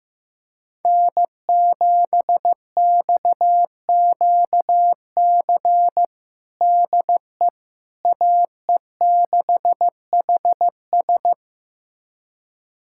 Morse code, heard as a keyed tone, about 15 wpm, 700 Hz. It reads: N7XQC DE AE6HS
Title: USA Callsign DE Callsign - 5 Characters - 15wpm